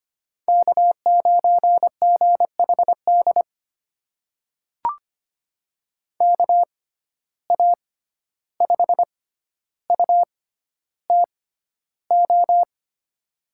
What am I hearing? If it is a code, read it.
Morse code, heard as a keyed tone, about 25 wpm, spaced as for 12 wpm, 700 Hz. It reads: K9GHD KA5UTO
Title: USA Call Signs - 6x Character Spacing - 25wpm